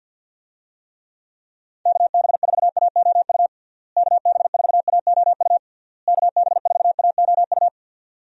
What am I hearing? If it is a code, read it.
Morse code, heard as a keyed tone, about 52 wpm, 700 Hz. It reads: KB4AOU KB4AOU KB4AOU
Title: USA Call Signs - 6 Characters - Repeated 3x - 50wpm